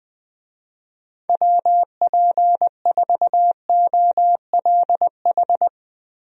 Morse code, heard as a keyed tone, about 20 wpm, 700 Hz. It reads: WP4OLH
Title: USA Call Signs - 6 Characters - Repeated 3x - 20wpm